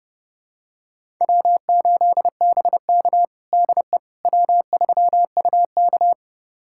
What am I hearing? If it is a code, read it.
Morse code, heard as a keyed tone, about 30 wpm, 700 Hz. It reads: W8BK DE W3UK
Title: USA Callsign DE Callsign - 4 Characters - 30wpm